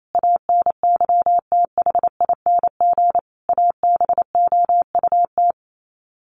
Morse code, heard as a keyed tone, about 28 wpm, 700 Hz. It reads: ANYTHING ABOUT
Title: Sets of 5 Words 28wpm